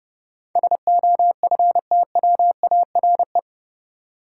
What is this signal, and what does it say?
Morse code, heard as a keyed tone, about 30 wpm, 700 Hz. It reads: SOFTWARE
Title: Sentences from Top 1000 Words - Review Part 3 - New Word Drill - 30wpm